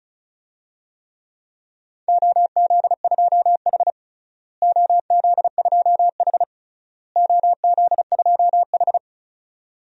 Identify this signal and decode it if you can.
Morse code, heard as a keyed tone, about 36 wpm, 700 Hz. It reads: OZ2H OZ2H OZ2H
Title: DXCC-100 Call Signs - 4 Characters - Repeated 3x - 35wpm